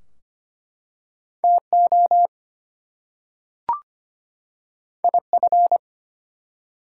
Morse code, heard as a keyed tone, about 25 wpm, 700 Hz. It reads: TO IF
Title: Two Letter Words - 25wpm